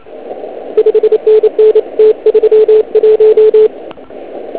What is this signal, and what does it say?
Morse code, heard as a keyed tone, about 28 wpm, 430 Hz. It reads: 5NN T31